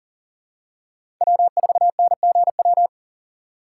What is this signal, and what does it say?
Morse code, heard as a keyed tone, about 40 wpm, 700 Hz. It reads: W4NGW
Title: USA Call Signs - 5 Characters - 4x Character Spacing - 40wpm